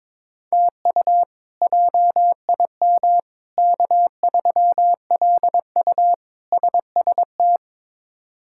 Morse code, heard as a keyed tone, about 22 wpm, 700 Hz. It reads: TU JIM K3LU SST
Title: K1USN SST QSO Practice - 22wpm